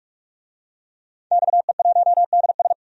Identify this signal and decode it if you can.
Morse code, heard as a keyed tone, about 46 wpm, 700 Hz. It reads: XE1DS
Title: DXCC-100 Call Signs - 5 Characters - 45wpm